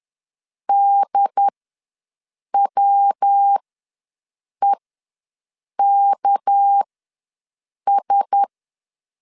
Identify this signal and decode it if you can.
Morse code, heard as a keyed tone, about 10 wpm, spaced as for 6 wpm, 800 Hz. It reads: DWEKS
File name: LEZIONI / 6 PKX0 / 6 PK ws 10wpm 3745